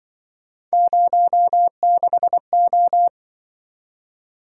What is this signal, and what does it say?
Morse code, heard as a keyed tone, about 24 wpm, 700 Hz. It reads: 06O
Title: Speed-Racing - Three Letter-Number Combinations - 20wpm